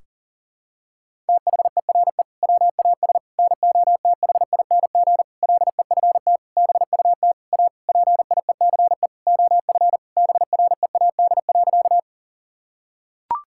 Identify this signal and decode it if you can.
Morse code, heard as a keyed tone, about 40 wpm, 700 Hz. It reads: THERE WAS NOTHING LEFT BUT A PIECE OF BREAD.
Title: Sentences from Top 1600 Words 40wpm